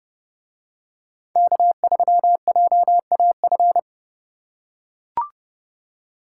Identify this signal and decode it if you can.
Morse code, heard as a keyed tone, about 31 wpm, 700 Hz. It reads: K3JAF